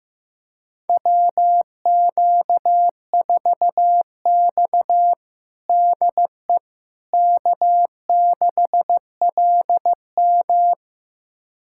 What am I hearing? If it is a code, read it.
Morse code, heard as a keyed tone, about 15 wpm, 700 Hz. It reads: WQ4X DE K6LM
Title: USA Callsign DE Callsign - 4 Characters - 15wpm